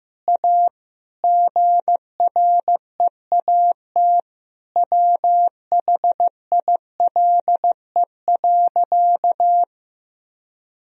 Morse code, heard as a keyed tone, about 15 wpm, 700 Hz. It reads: A GREAT WHILE.
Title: Sentences from Top 1000 Words - Review Part 1 - 15wpm